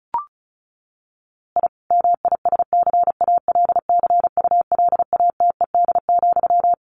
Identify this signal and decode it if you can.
Morse code, heard as a keyed tone, about 34 wpm, 700 Hz. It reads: I MISCALCULATED,
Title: A Sound Of Thunder 35wpm